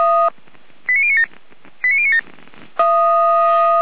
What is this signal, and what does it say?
Five-tone selective call (ZVEI), heard as quick runs of five tones, tones of about 80 ms. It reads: 89097 89097
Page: SDR/ZVEI